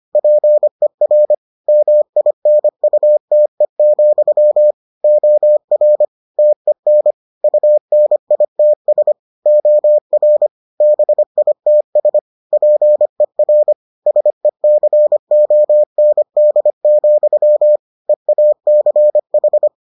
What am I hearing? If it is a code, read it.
Morse code, heard as a keyed tone, about 25 wpm, 600 Hz. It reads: PER MINUTE, OR TEN UNITS OR BITS PER SECOND, EACH